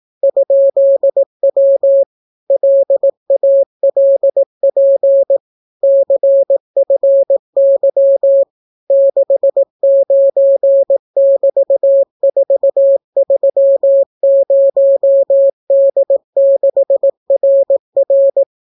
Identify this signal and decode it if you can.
Morse code, heard as a keyed tone, about 18 wpm, 550 Hz. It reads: ?W LALP CFY 69<BT>430D6RR